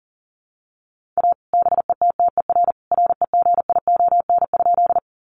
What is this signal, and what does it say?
Morse code, heard as a keyed tone, about 40 wpm, 700 Hz. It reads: A BETTER REGION?
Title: Sentences from Top 1500 Words 40wpm